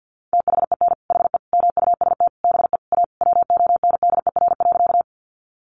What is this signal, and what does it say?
Morse code, heard as a keyed tone, about 50 wpm, 700 Hz. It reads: THEN HE MUST BE A WONDER.
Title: Sentences from Top 500 Words 50wpm